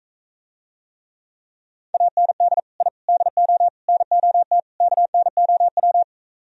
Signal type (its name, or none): Morse code